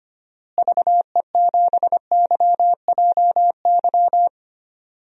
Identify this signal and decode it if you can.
Morse code, heard as a keyed tone, about 25 wpm, 700 Hz. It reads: VE7YJY
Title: DXCC-100 Call Signs - 6 Characters - Repeated 2x - 25wpm